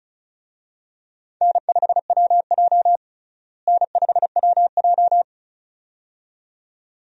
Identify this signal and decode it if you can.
Morse code, heard as a keyed tone, about 34 wpm, 700 Hz. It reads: N5WJ N5WJ
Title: USA Call Signs - 4 Characters - Repeated 2x - 35wpm